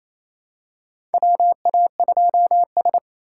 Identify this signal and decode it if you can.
Morse code, heard as a keyed tone, about 28 wpm, 700 Hz. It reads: WA2S
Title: USA Call Signs - 4 Characters - 2x Character Spacing - 28wpm